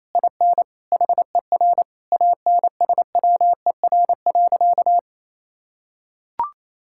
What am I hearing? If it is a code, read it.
Morse code, heard as a keyed tone, about 28 wpm, 700 Hz. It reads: IN HER ANSWER.
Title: Sentences from Top 1600 Words 28wpm